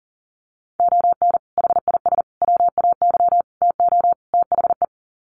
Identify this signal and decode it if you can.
Morse code, heard as a keyed tone, about 40 wpm, 700 Hz. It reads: ON HIS WAY TO THE
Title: Sets of 5 Words Encore2 40wpm